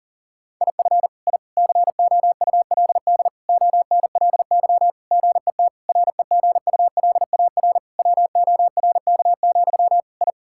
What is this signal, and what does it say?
Morse code, heard as a keyed tone, about 37 wpm, 700 Hz. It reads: IF I COULD ONLY GET REGULAR WORK, I
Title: Sentences from Top 1000 Words - Review Part 6 - 40wpm